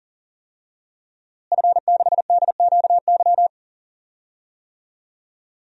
Morse code, heard as a keyed tone, about 40 wpm, 700 Hz. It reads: F6DQY
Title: DXCC-100 Call Signs - 5 Characters - Repeated 3x - 40wpm